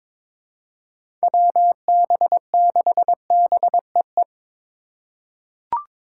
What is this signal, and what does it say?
Morse code, heard as a keyed tone, about 22 wpm, 700 Hz. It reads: WB6BEE
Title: K1USN SST QSO Practice - 22wpm